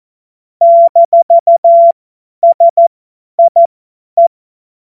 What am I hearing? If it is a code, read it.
Morse code, heard as a keyed tone, about 14 wpm, spaced as for 10 wpm, 680 Hz. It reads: -SIE
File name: Witz-22-Juli-2025-50-F70